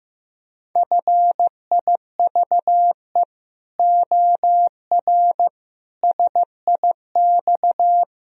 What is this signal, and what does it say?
Morse code, heard as a keyed tone, about 16 wpm, 700 Hz. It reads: FIVE OR SIX
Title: Sentences from Top 1000 Words - Review Part 3 - 15wpm